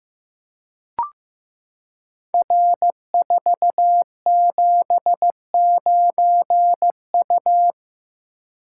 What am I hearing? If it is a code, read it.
Morse code, heard as a keyed tone, about 15 wpm, 700 Hz. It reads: R479U
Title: Code Groups - 5 Letter:Numbers - 15wpm